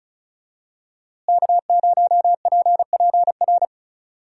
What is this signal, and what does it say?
Morse code, heard as a keyed tone, about 36 wpm, 700 Hz. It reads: K0PPR
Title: USA Call Signs - 5 Characters - Repeated 3x - 35wpm